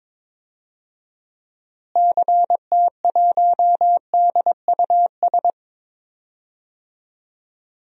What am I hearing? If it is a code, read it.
Morse code, heard as a keyed tone, about 22 wpm, 700 Hz. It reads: CT1DUS
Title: DXCC-100 Call Signs - 6 Characters - Repeated 2x - 22wpm